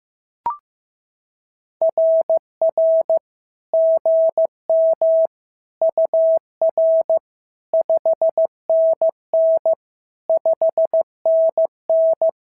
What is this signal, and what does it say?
Morse code, heard as a keyed tone, about 14 wpm, 650 Hz. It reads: RR GM UR 5NN 5NN
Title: SOTA QSO Practice 15wpm